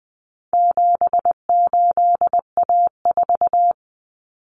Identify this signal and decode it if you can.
Morse code, heard as a keyed tone, about 20 wpm, 700 Hz. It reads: 78A4
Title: Common Cut Numbers - 4-Digit 20wpm